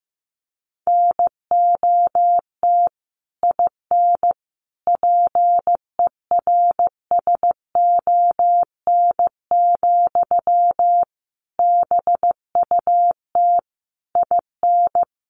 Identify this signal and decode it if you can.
Morse code, heard as a keyed tone, about 15 wpm, 700 Hz. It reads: NOT IN PERSON, BUT IN
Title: Sentences from Top 1400 Words 15wpm